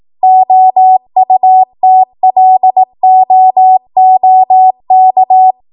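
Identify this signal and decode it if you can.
Morse code, heard as a keyed tone, about 18 wpm, 750 Hz. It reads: OUTLOOK
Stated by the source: ARRL 18 wpm Morse code practice archive transmission.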